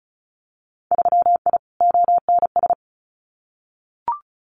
Morse code, heard as a keyed tone, about 34 wpm, 700 Hz. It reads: 3I ONS
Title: Winter Field Day Exchanges Encore 35wpm